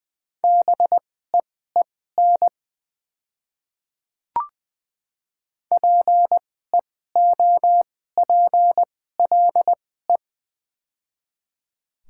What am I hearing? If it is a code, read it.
Morse code, heard as a keyed tone, about 20 wpm, spaced as for 14 wpm, 700 Hz. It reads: BEEN PEOPLE
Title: 2x Character Spacing - Top 100 Words - 20wpm